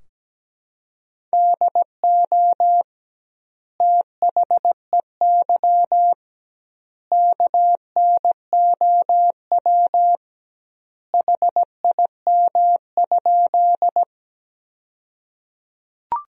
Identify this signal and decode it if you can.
Morse code, heard as a keyed tone, about 17 wpm, 700 Hz. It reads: DO THEY KNOW HIM?
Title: 2x Word-Spacing - Sentences from Top 100 Words - 17wpm